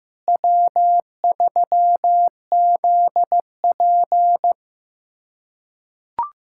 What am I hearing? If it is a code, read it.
Morse code, heard as a keyed tone, about 16 wpm, 700 Hz. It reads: W3ZP
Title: USA Callsign DE Callsign - 4 Characters - 15wpm